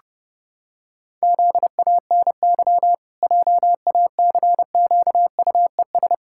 Morse code, heard as a keyed tone, about 30 wpm, 700 Hz. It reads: ZANY JACQUES
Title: Pangram Challenge v2 - 30wpm